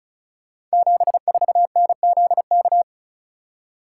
Morse code, heard as a keyed tone, about 36 wpm, 690 Hz. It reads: 74NZK
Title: Code Groups - 5 Letter:Numbers - 35wpm